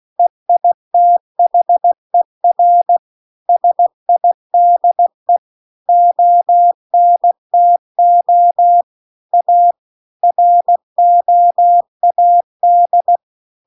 Morse code, heard as a keyed tone, about 16 wpm, 700 Hz. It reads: EITHER SIDE ONTO A ROAD